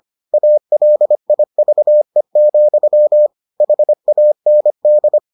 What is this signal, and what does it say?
Morse code, heard as a keyed tone, about 25 wpm, 600 Hz. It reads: ALIVE, HAND